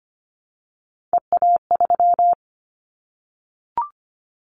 Morse code, heard as a keyed tone, about 25 wpm, 700 Hz. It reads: EA3
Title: Callsign Prefixes for Top 25 DXCC - Repeated 3x - 25wpm